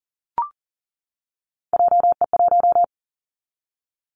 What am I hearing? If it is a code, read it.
Morse code, heard as a keyed tone, about 41 wpm, 700 Hz. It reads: JE1
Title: Callsign Prefixes for Top 25 DXCC - 40wpm